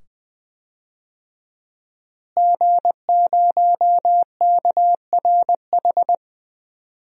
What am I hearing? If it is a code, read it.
Morse code, heard as a keyed tone, about 20 wpm, 700 Hz. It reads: G0KRH
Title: DXCC-100 Call Signs - 5 Characters - Repeated 2x - 20wpm